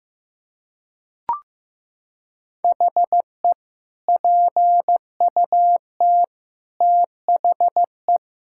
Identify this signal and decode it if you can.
Morse code, heard as a keyed tone, about 15 wpm, 700 Hz. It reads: HE PUT THE
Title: Sentences from Top 1600 Words 15wpm